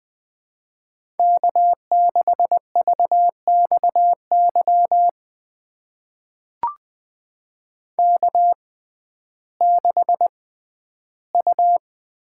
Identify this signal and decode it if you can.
Morse code, heard as a keyed tone, about 20 wpm, spaced as for 12 wpm, 700 Hz. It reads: K6VXY K6U
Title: USA Call Signs - 5 Characters - 6x Character Spacing - 20wpm